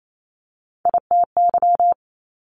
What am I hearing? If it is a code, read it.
Morse code, heard as a keyed tone, about 28 wpm, 700 Hz. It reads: ITY
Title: Multi-letter Phonemes Quick Fire 28wpm